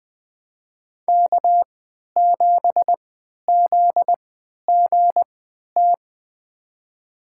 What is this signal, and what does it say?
Morse code, heard as a keyed tone, about 20 wpm, spaced as for 11 wpm, 700 Hz. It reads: K7ZGT
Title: USA Call Signs - 5 Characters - 3x Character Spacing - 20wpm